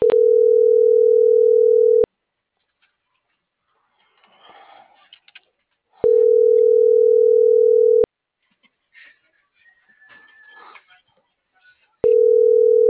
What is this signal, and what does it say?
Ringback tone (North America), 440 + 480 Hz.